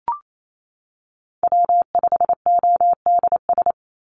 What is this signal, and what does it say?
Morse code, heard as a keyed tone, about 28 wpm, 700 Hz. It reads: W5ODS
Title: NA SOTA Activator Callsigns 28wpm